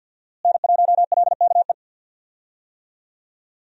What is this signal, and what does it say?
Morse code, heard as a keyed tone, about 50 wpm, 700 Hz. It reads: N1LKE
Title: USA Call Signs - 5 Characters - Repeated 2x - 50wpm